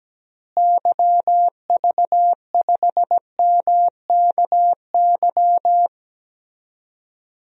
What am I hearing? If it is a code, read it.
Morse code, heard as a keyed tone, about 17 wpm, 700 Hz. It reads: YV5MKY
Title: DXCC-100 Call Signs - 6 Characters - Repeated 3x - 17wpm